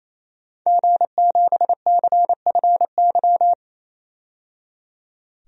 Morse code, heard as a keyed tone, about 28 wpm, 700 Hz. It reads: G7CFY
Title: DXCC-100 Call Signs - 5 Characters - 28wpm